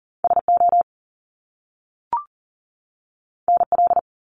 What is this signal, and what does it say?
Morse code, heard as a keyed tone, about 40 wpm, 700 Hz. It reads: SO NL
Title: Top Two Letter Combinations in Words 40wpm